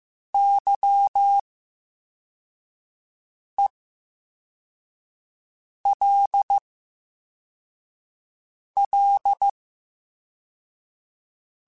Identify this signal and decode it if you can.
Morse code, heard as a keyed tone, about 15 wpm, spaced as for 4 wpm, 800 Hz. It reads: YELL
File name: Special INGs / Special INGs-mp3s / Slow ing @13